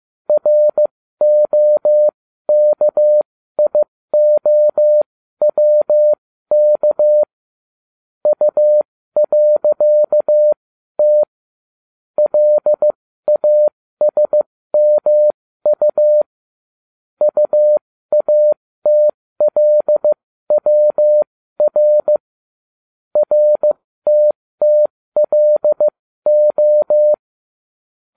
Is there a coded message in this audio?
Yes: Morse code from 0.3 to 27.1 s, heard as a keyed tone, about 15 wpm, spaced as for 13 wpm, 600 Hz. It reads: ROKIOWK U.T LASMU UATLWR RTTLO